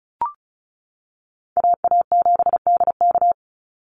Morse code, heard as a keyed tone, about 34 wpm, 700 Hz. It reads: AA7DK
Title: NA SOTA Chaser Callsigns 35wpm